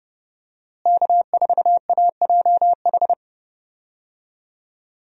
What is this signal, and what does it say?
Morse code, heard as a keyed tone, about 30 wpm, 700 Hz. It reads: K4AJH
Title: USA Call Signs - 5 Characters - Repeated 2x - 30wpm